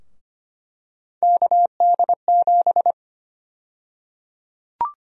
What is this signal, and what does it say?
Morse code, heard as a keyed tone, about 25 wpm, 700 Hz. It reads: KD7
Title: USA Call Sign Prefixes - 25wpm